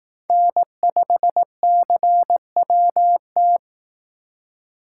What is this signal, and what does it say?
Morse code, heard as a keyed tone, about 19 wpm, 700 Hz. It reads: N5CWT
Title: Speed Racing - USA 1x3 Call Signs 15wpm